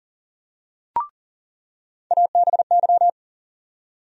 Morse code, heard as a keyed tone, about 40 wpm, 700 Hz. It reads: ABY
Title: USA Call Sign Postfixes - 3 Letters 40wpm